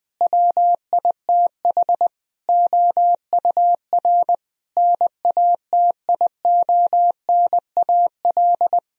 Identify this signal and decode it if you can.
Morse code, heard as a keyed tone, about 20 wpm, 700 Hz. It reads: WITH OUR NATIONAL
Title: Sentences from Top 1000 Words - Review Part 4 - 20wpm